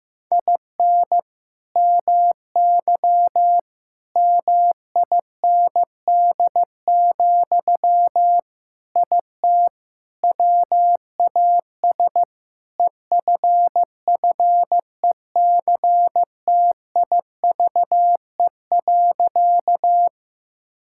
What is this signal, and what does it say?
Morse code, heard as a keyed tone, about 16 wpm, 700 Hz. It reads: IN MY MIND, IT WAS EFFECTIVE.
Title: Sentences from Top 1000 Words - Review Part 6 - 15wpm